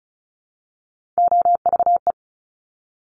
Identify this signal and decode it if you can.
Morse code, heard as a keyed tone, about 36 wpm, 700 Hz. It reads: OVE
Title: Top Three Letter Combinations in Words 35wpm